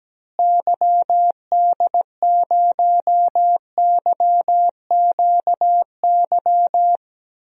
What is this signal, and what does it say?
Morse code, heard as a keyed tone, about 17 wpm, 700 Hz. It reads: YD0YQY
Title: DXCC-100 Call Signs - 6 Characters - Repeated 2x - 17wpm